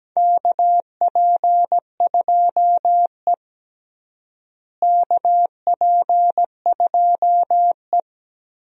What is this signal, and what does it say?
Morse code, heard as a keyed tone, about 17 wpm, 700 Hz. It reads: KP2E KP2E
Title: DXCC-100 Call Signs - 4 Characters - Repeated 3x - 17wpm